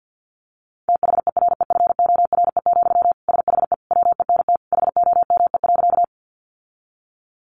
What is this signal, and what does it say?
Morse code, heard as a keyed tone, about 50 wpm, 700 Hz. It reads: THEREFORE, SHE WENT HOME.
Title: Sentences from Top 800 Words 50wpm